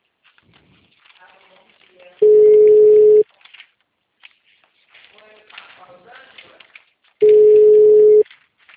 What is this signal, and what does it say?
Ringback tone (Europe, 425 Hz), with speech.
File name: out- -2007- - - .22830